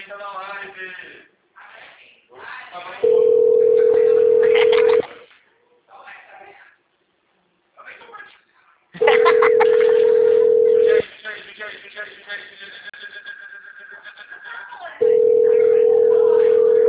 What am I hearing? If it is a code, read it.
Ringback tone (North America), 440 + 480 Hz, with speech.
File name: out- -2015- - - .68652